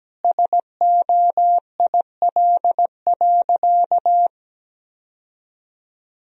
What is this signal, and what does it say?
Morse code, heard as a keyed tone, about 17 wpm, 700 Hz. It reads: SOIL.
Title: Sentences from Top 1000 Words - Review Part 4 - 17wpm